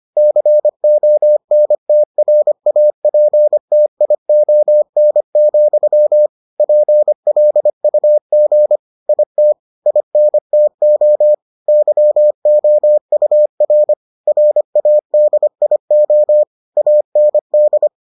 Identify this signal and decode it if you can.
Morse code, heard as a keyed tone, about 25 wpm, 600 Hz. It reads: CONTRAPTION, PLUG IT INTO YOUR RADIO AND